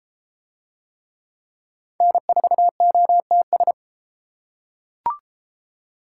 Morse code, heard as a keyed tone, about 32 wpm, 700 Hz. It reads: N4OTS